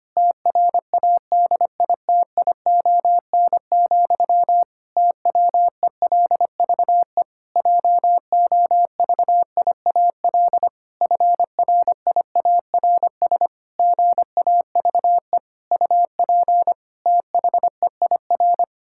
Morse code, heard as a keyed tone, about 25 wpm, 700 Hz. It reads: TRADITION, TWELVE JOVIAL FRIARS GAVE UP THEIR
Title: Pangram Challenge v2 - 25wpm